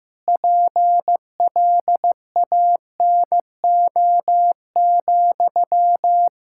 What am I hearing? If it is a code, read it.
Morse code, heard as a keyed tone, about 15 wpm, 700 Hz. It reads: PLANO,
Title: Practice QSO - Exchange 1 - Beginning - 15wpm